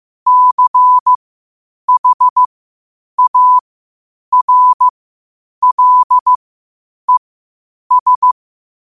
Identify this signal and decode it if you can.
Morse code, heard as a keyed tone, about 15 wpm, spaced as for 8 wpm, 1000 Hz. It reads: CHARLES
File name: qotdfiles / 8wpm